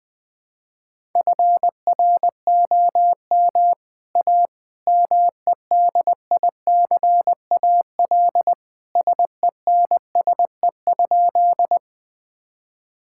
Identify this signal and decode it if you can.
Morse code, heard as a keyed tone, about 21 wpm, 700 Hz. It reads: FROM A MEDICAL SENSE?
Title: Sentences from Top 1000 Words - Review Part 5 - 20wpm